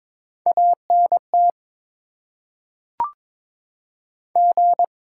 Morse code, heard as a keyed tone, about 22 wpm, 690 Hz. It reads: ANT G
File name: Top 90 QSO Elements - 8x Character Spacing - 22wpm